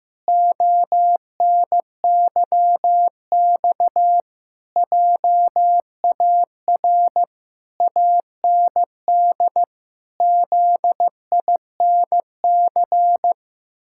Morse code, heard as a keyed tone, about 15 wpm, 700 Hz. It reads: ONYX JAR AND ZINC
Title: Pangram Challenge v2 - 15wpm